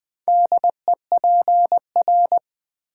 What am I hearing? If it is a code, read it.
Morse code, heard as a keyed tone, about 20 wpm, 700 Hz. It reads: DEPR
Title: Build A Word - 4 Letters At A Time - 10-Letter Words - 20wpm